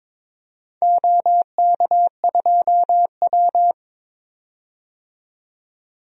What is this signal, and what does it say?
Morse code, heard as a keyed tone, about 22 wpm, 700 Hz. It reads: OK2W
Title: DXCC-100 Call Signs - 4 Characters - 22wpm